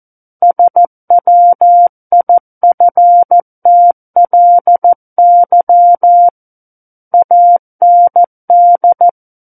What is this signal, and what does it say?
Morse code, heard as a keyed tone, about 14 wpm, 700 Hz. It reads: SWIFTLY AND
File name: Book One / 05-Chapter 5 - 14 WPM